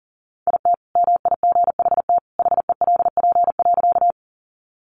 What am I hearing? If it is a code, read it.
Morse code, heard as a keyed tone, about 40 wpm, 700 Hz. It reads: IT MIGHT HELP.
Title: Sentences from Top 300 Words 40wpm